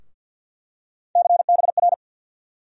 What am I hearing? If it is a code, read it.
Morse code, heard as a keyed tone, about 49 wpm, 700 Hz. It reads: CBR
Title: USA Call Sign Postfixes - 3 Letters 50wpm